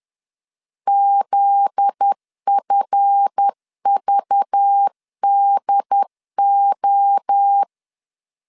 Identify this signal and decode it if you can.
Morse code, heard as a keyed tone, about 10 wpm, 800 Hz. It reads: ZFVDO
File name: LEZIONI / 7 GF9Z / 7 GF9Z 10wpm 2611